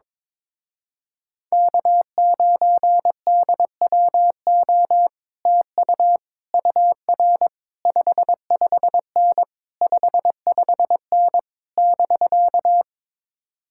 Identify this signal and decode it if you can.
Morse code, heard as a keyed tone, about 22 wpm, 700 Hz. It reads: K9DWO TU UR 55N 55N <BK>
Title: Speed-Racing Common QSO Phrases Encore - 22wpm